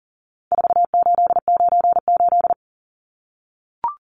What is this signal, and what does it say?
Morse code, heard as a keyed tone, about 40 wpm, 700 Hz. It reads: 4898
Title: Numbers - 4-Digit 40wpm